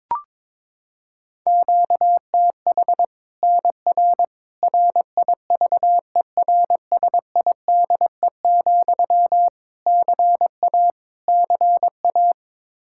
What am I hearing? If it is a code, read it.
Morse code, heard as a keyed tone, about 22 wpm, 700 Hz. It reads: QTH NR RIVERSIDE, CA CA
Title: Practice QSO - Exchange 1 - Beginning - 22wpm